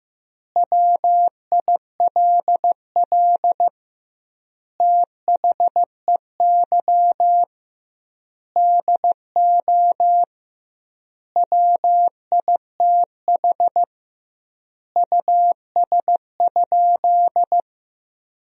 Morse code, heard as a keyed tone, about 15 wpm, 700 Hz. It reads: WILL THEY DO WITH US?
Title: 2x Word-Spacing - Sentences from Top 200 Words - 15wpm